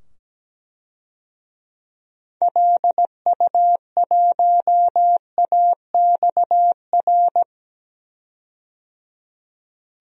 Morse code, heard as a keyed tone, about 17 wpm, 700 Hz. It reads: LU1AXR
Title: DXCC-100 Call Signs - 6 Characters - 17wpm